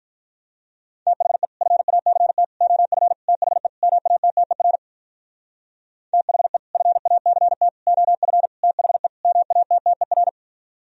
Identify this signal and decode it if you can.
Morse code, heard as a keyed tone, about 43 wpm, 700 Hz. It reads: THE FACT OF THE MATTER THE FACT OF THE MATTER
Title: Speed-Racing - Sets of 5 Words 35wpm